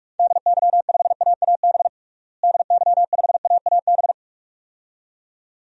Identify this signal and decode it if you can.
Morse code, heard as a keyed tone, about 46 wpm, 700 Hz. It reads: DY5AAB DY5AAB
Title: DXCC-100 Call Signs - 6 Characters - Repeated 2x - 45wpm